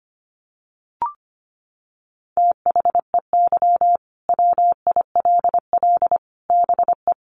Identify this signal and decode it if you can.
Morse code, heard as a keyed tone, about 25 wpm, 700 Hz. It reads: THEY WILL BE
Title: Sets of 5 Words Encore 25wpm